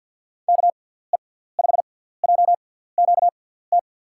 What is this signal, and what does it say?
Morse code, heard as a keyed tone, about 46 wpm, spaced as for 17 wpm, 700 Hz. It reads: KE5JQT
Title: USA Call Signs - 6 Characters - 6x Character Spacing - 50wpm